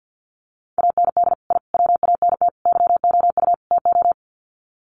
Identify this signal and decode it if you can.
Morse code, heard as a keyed tone, about 52 wpm, 700 Hz. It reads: AND I WANT YOU TO
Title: Sets of 5 Words Encore 50wpm